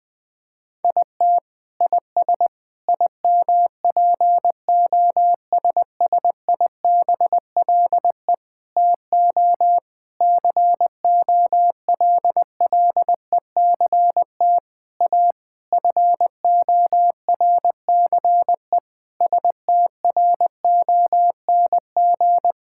Morse code, heard as a keyed tone, about 20 wpm, 700 Hz. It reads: IT IS IMPOSSIBLE TO COLLECT A FORCE STRONG
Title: Sentences from Top 1600 Words 20wpm